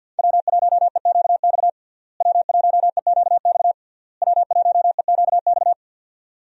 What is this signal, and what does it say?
Morse code, heard as a keyed tone, about 47 wpm, 700 Hz. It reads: W1EQX W1EQX W1EQX
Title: USA Call Signs - 5 Characters - Repeated 3x - 50wpm